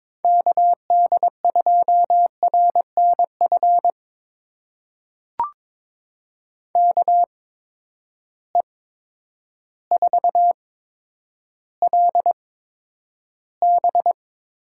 Morse code, heard as a keyed tone, about 21 wpm, spaced as for 10 wpm, 700 Hz. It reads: KD2RNF KE4LB
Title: USA Call Signs - 6 Characters - 8x Character Spacing - 22wpm